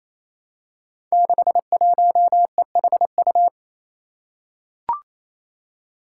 Morse code, heard as a keyed tone, about 29 wpm, 700 Hz. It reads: 61EHU
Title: Five Challenging Letter-Number Combinations 28wpm